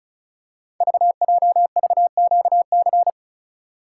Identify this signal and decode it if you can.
Morse code, heard as a keyed tone, about 36 wpm, 700 Hz. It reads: VJVQC
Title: Code Groups - 5 Letters - 35wpm